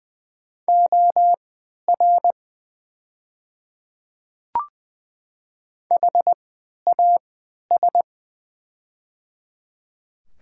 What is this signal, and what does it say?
Morse code, heard as a keyed tone, about 20 wpm, spaced as for 11 wpm, 690 Hz. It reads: OR HAS
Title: 3x Character Spacing - Top 200 Words - 20wpm